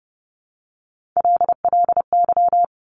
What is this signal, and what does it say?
Morse code, heard as a keyed tone, about 30 wpm, 700 Hz. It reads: LLY
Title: Top Three Letter Combinations in Words 30wpm